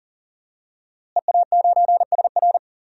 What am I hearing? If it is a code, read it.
Morse code, heard as a keyed tone, about 40 wpm, 700 Hz. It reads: EA9SR
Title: DXCC-100 Call Signs - 5 Characters - Repeated 2x - 40wpm